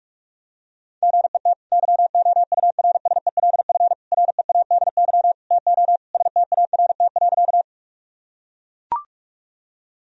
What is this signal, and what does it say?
Morse code, heard as a keyed tone, about 41 wpm, 700 Hz. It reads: GET YOURSELF READY TO START.
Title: Sentences from Top 1000 Words - Review Part 1 - 45wpm